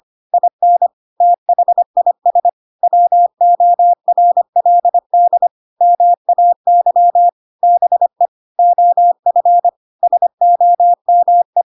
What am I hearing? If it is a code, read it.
Morse code, heard as a keyed tone, about 25 wpm, 700 Hz. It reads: IN THIS WORLD MAY BE OF SOME
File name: Pinocchio / Pinocchio ch7 25 25 ik1hge